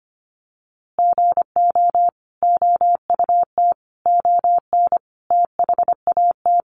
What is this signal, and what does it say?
Morse code, heard as a keyed tone, about 25 wpm, 700 Hz. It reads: GO OUT ON THAT
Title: A Sound Of Thunder 25wpm